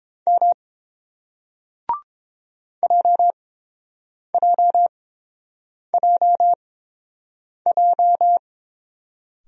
Morse code, heard as a keyed tone, about 30 wpm, spaced as for 7 wpm, 700 Hz. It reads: M JJJJ